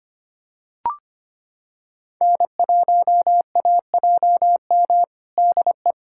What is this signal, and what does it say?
Morse code, heard as a keyed tone, about 25 wpm, 700 Hz. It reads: N1AJM DE
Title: USA Callsign DE Callsign 25wpm